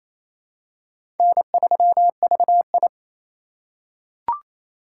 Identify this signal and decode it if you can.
Morse code, heard as a keyed tone, about 28 wpm, 700 Hz. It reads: N3VI
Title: USA Call Signs - 4 Characters - 4x Character Spacing - 28wpm